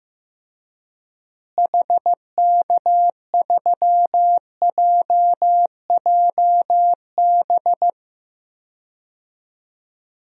Morse code, heard as a keyed tone, about 15 wpm, 700 Hz. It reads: HK3JJB
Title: DXCC-100 Call Signs - 6 Characters - Repeated 2x - 15wpm